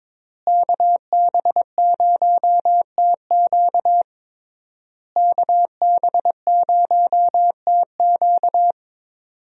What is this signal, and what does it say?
Morse code, heard as a keyed tone, about 22 wpm, 700 Hz. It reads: KB0TQ KB0TQ
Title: USA Call Signs - 5 Characters - Repeated 3x - 22wpm